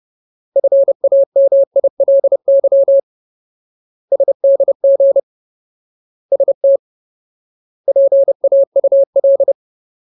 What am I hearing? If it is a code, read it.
Morse code, heard as a keyed tone, about 30 wpm, 550 Hz. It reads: FAMILY SDG ST PAUL
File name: Ultimate Word List / 30WPM Ultimate